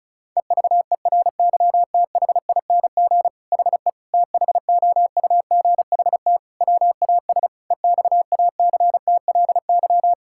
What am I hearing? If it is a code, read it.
Morse code, heard as a keyed tone, about 34 wpm, 700 Hz. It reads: EVERYTHING HE THOUGHT WAS EXACTLY
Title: Sentences from Top 1000 Words - Review Part 6 - 35wpm